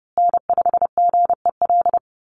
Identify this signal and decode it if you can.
Morse code, heard as a keyed tone, about 30 wpm, 700 Hz. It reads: N5GEL
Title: USA Call Signs Repeated 2x 30wpm